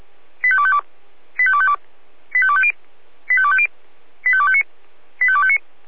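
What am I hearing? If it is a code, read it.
Five-tone selective call (ZVEI), heard as quick runs of five tones, tones of about 70 ms. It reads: 85252 85252 85250 85250 85259 85259
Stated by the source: ZVEI five-tone selective call 85201 received by SDR.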